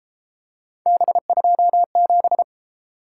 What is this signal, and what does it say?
Morse code, heard as a keyed tone, about 30 wpm, 700 Hz. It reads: B27